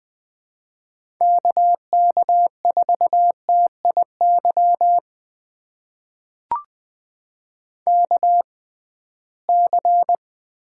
Morse code, heard as a keyed tone, about 20 wpm, spaced as for 15 wpm, 700 Hz. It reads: KK4TIY KC
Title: USA Call Signs - 6 Characters - 6x Character Spacing - 20wpm